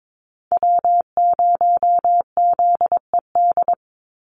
Morse code, heard as a keyed tone, about 22 wpm, 700 Hz. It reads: W0ZED
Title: USA Call Signs 22wpm